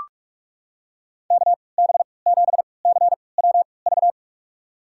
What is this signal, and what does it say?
Morse code, heard as a keyed tone, about 43 wpm, spaced as for 25 wpm, 700 Hz. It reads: KB7CWV
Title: USA Call Signs - 3x Character Spacing - 45wpm